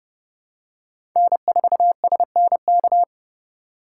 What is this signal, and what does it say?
Morse code, heard as a keyed tone, about 30 wpm, 700 Hz. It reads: N4SNK
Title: USA Call Signs - 5 Characters - 3x Character Spacing - 30wpm